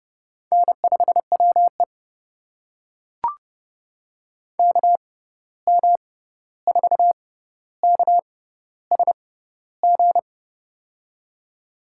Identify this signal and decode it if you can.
Morse code, heard as a keyed tone, about 31 wpm, spaced as for 14 wpm, 700 Hz. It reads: N5WE KM4KSG
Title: USA Call Signs - 6x Character Spacing - 30wpm